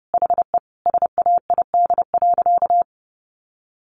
Morse code, heard as a keyed tone, about 30 wpm, 700 Hz. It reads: HE SAID.
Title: Sentences from Top 1000 Words 30wpm